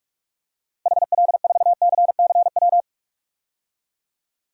Morse code, heard as a keyed tone, about 46 wpm, 700 Hz. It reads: HL4CCW
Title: DXCC-100 Call Signs - 6 Characters - Repeated 3x - 45wpm